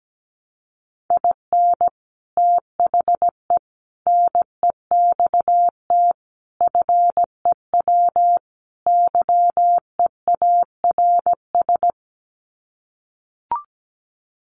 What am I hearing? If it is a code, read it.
Morse code, heard as a keyed tone, about 17 wpm, 700 Hz. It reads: IN THE NEXT FEW YEARS
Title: Sets of 5 Words Encore 17wpm